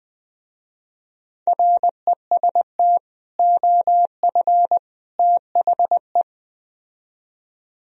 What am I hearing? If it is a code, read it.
Morse code, heard as a keyed tone, about 20 wpm, 700 Hz. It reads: REST OF THE
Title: Speed-Racing - Sets of 3 Words 17wpm